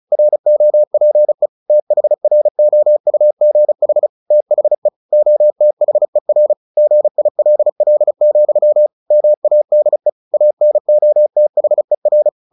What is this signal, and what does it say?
Morse code, heard as a keyed tone, about 34 wpm, 600 Hz. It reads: ROPE THROUGH THE OTHER GILL, MADE ANOTHER
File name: the old man and the sea 2 35wpm